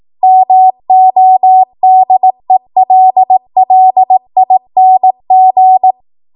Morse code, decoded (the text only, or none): MODELLING